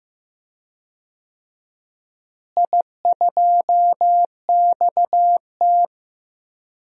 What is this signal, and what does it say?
Morse code, heard as a keyed tone, about 15 wpm, 700 Hz. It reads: I2XT
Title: DXCC-100 Call Signs - 4 Characters - Repeated 2x - 15wpm